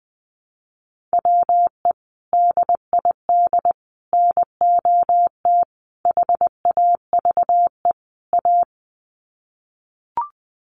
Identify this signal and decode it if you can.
Morse code, heard as a keyed tone, about 20 wpm, 700 Hz. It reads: WE DID NOT HAVE A
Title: Sets of 5 Words Encore2 20wpm